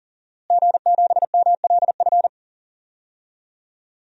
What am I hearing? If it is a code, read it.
Morse code, heard as a keyed tone, about 40 wpm, 700 Hz. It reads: G7MLF
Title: DXCC-100 Call Signs - 5 Characters - Repeated 3x - 40wpm